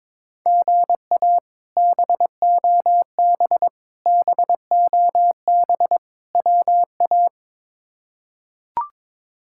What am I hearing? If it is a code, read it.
Morse code, heard as a keyed tone, about 22 wpm, 700 Hz. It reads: GA BOB BOB WA
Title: K1USN SST QSO Practice - 22wpm